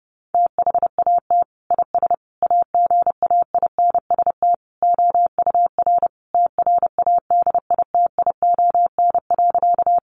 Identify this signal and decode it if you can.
Morse code, heard as a keyed tone, about 30 wpm, 700 Hz. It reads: THAT IS AGAINST OUR TRADITION.
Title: Sentences from Top 1500 Words 30wpm